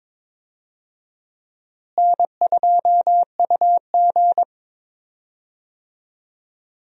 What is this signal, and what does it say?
Morse code, heard as a keyed tone, about 22 wpm, 700 Hz. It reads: N2UG
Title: USA Call Signs - 4 Characters - Repeated 2x - 22wpm